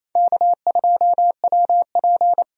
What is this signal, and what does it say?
Morse code, heard as a keyed tone, about 28 wpm, 700 Hz. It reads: K2WP
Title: USA Call Signs - 4 Characters - 28wpm